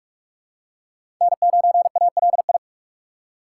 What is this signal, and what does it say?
Morse code, heard as a keyed tone, about 42 wpm, 700 Hz. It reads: N9ALI
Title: USA Call Signs - 5 Characters - 4x Character Spacing - 45wpm